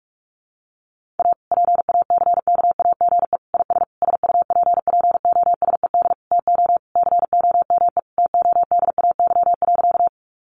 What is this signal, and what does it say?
Morse code, heard as a keyed tone, about 44 wpm, 700 Hz. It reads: A PACKAGE IS SUPPOSED TO COME TODAY.
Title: Easy Phrases 45wpm